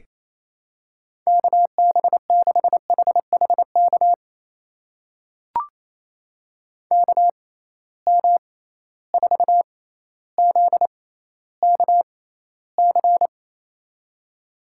Morse code, heard as a keyed tone, about 28 wpm, spaced as for 14 wpm, 700 Hz. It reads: KB6HHK KM4ZKC
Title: USA Call Signs - 6 Characters - 6x Character Spacing - 28wpm